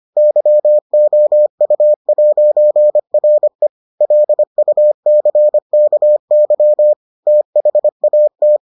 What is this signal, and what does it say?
Morse code, heard as a keyed tone, about 25 wpm, 600 Hz. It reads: YOU'RE LUCKY THAT